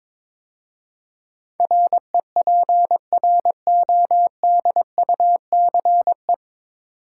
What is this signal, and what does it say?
Morse code, heard as a keyed tone, about 22 wpm, 700 Hz. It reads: REPRODUCE
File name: 9 Letter Words - First 3 Letters Given - 22wpm